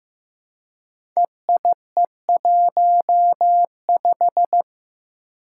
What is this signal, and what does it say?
Morse code, heard as a keyed tone, about 15 wpm, 700 Hz. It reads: EIE15
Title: Five Challenging Letter-Number Combinations 15wpm